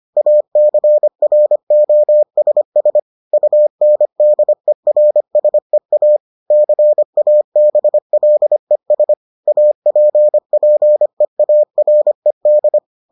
Morse code, heard as a keyed tone, about 25 wpm, 600 Hz. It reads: ACROSS UNDERSEA CABLES APPEARED